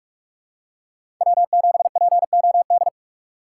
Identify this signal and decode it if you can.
Morse code, heard as a keyed tone, about 43 wpm, 700 Hz. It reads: W7POD
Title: USA Call Signs - 5 Characters - 6x Character Spacing - 45wpm